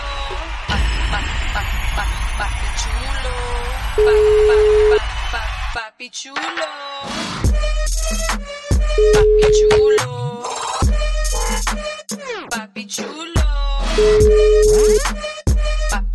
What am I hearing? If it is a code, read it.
Ringback tone (Europe, 425 Hz), with speech.